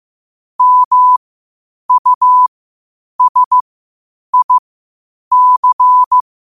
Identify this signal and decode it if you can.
Morse code, heard as a keyed tone, about 15 wpm, spaced as for 8 wpm, 1000 Hz. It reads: MUSIC
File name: qotdfiles / 8wpm